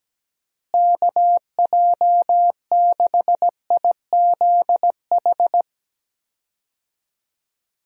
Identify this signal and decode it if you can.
Morse code, heard as a keyed tone, about 17 wpm, 700 Hz. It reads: KJ6IZH
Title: USA Call Signs - 6 Characters - Repeated 2x - 17wpm